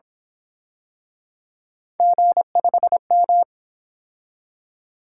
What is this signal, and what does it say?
Morse code, heard as a keyed tone, about 27 wpm, 700 Hz. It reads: G5M
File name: Speed-Racing - Three Letter-Number Combinations - 22wpm